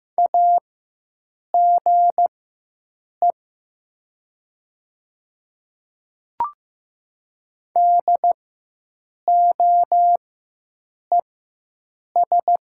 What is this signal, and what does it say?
Morse code, heard as a keyed tone, about 15 wpm, spaced as for 7 wpm, 700 Hz. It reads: AGE DOES
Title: 4x Character Spacing - Top 500 Words - 15wpm